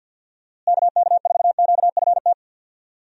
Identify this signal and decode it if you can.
Morse code, heard as a keyed tone, about 50 wpm, 700 Hz. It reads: KK4QFT
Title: USA Call Signs - 6 Characters - Repeated 3x - 50wpm